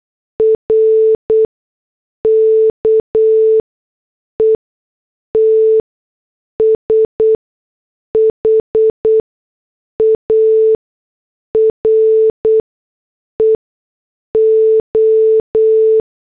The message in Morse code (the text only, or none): RKET SHARE O